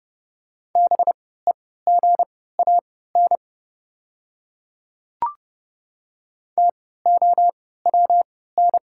Morse code, heard as a keyed tone, about 30 wpm, spaced as for 17 wpm, 700 Hz. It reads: BEGAN TOWN
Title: 3x Character Spacing - Top 500 Words - 30wpm